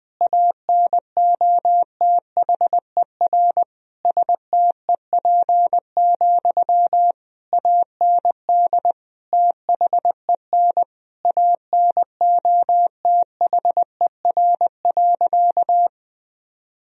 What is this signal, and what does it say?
Morse code, heard as a keyed tone, about 20 wpm, 700 Hz. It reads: ANOTHER STEP, AND THEN ANOTHER.
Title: Sentences from Top 1000 Words - Review Part 2 - 20wpm